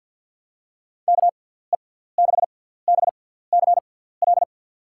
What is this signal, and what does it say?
Morse code, heard as a keyed tone, about 51 wpm, spaced as for 17 wpm, 700 Hz. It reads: KE6BCL
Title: USA Call Signs - 6x Character Spacing - 50wpm